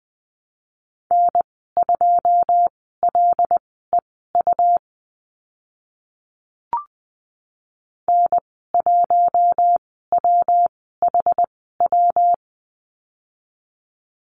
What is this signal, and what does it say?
Morse code, heard as a keyed tone, about 20 wpm, spaced as for 14 wpm, 700 Hz. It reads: N2LEU N1WHW
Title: 128 Lesson 21g - Number 6 - Call Signs 20wpm